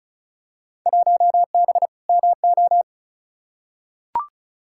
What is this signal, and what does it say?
Morse code, heard as a keyed tone, about 36 wpm, 700 Hz. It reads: 1B MO
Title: Field Day Exchanges - 35wpm